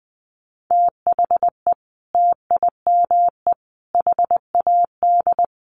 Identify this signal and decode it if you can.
Morse code, heard as a keyed tone, about 20 wpm, 700 Hz. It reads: THE TIME HAD
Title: Sentences from Top 100 Words 20wpm